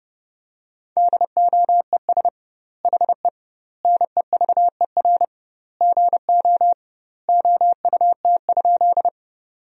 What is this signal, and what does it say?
Morse code, heard as a keyed tone, about 30 wpm, 700 Hz. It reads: DOES HE NEVER GO OUT?
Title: 2x Word-Spacing - Sentences from Top 300 Words - 30wpm